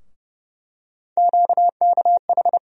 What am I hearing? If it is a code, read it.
Morse code, heard as a keyed tone, about 30 wpm, 700 Hz. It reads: QKH
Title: USA Call Sign Postfixes - 3 Letters 30wpm